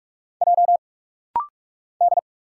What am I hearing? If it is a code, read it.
Morse code, heard as a keyed tone, about 46 wpm, 700 Hz. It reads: J D
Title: Single Letters - Rapid-Fire 45wpm